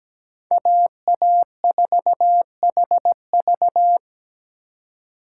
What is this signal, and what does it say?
Morse code, heard as a keyed tone, about 17 wpm, 700 Hz. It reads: AA4HV
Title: K1USN SST QSO Practice - 17wpm